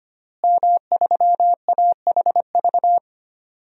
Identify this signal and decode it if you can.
Morse code, heard as a keyed tone, about 25 wpm, 700 Hz. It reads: M3AHV
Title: DXCC-100 Call Signs - 5 Characters - Repeated 2x - 25wpm